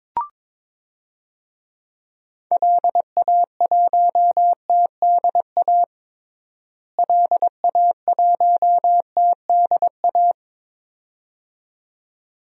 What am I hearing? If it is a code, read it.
Morse code, heard as a keyed tone, about 22 wpm, 700 Hz. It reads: LA1TDA LA1TDA
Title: DXCC-100 Call Signs - 6 Characters - Repeated 2x - 22wpm